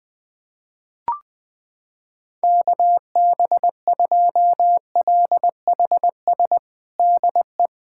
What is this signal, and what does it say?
Morse code, heard as a keyed tone, about 20 wpm, 700 Hz. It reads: KB2LHS DE
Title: USA Callsign DE Callsign 20wpm